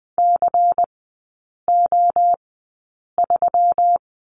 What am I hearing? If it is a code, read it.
Morse code, heard as a keyed tone, about 20 wpm, spaced as for 8 wpm, 700 Hz. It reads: CO3
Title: 145 Lesson 24e - Letter y - Three Characters 20wpm